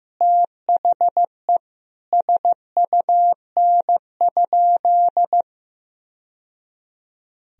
Morse code, heard as a keyed tone, about 16 wpm, 700 Hz. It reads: THE SUN?
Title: Sentences from Top 1000 Words - Review Part 1 - 15wpm